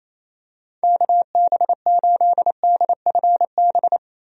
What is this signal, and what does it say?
Morse code, heard as a keyed tone, about 28 wpm, 700 Hz. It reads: KB8DFB
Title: USA Call Signs - 6 Characters - 28wpm